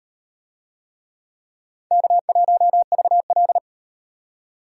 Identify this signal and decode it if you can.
Morse code, heard as a keyed tone, about 38 wpm, 700 Hz. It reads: K1VL